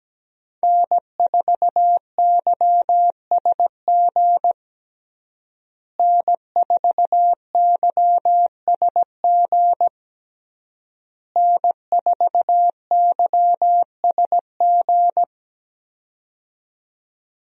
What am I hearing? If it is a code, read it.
Morse code, heard as a keyed tone, about 17 wpm, 700 Hz. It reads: N4YSG N4YSG N4YSG
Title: USA Call Signs - 5 Characters - Repeated 3x - 17wpm